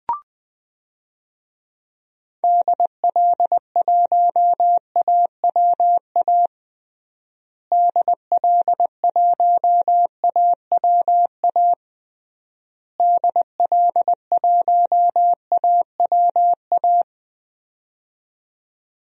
Morse code, heard as a keyed tone, about 20 wpm, 700 Hz. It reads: DL1AWA DL1AWA DL1AWA
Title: DXCC-100 Call Signs - 6 Characters - Repeated 3x - 20wpm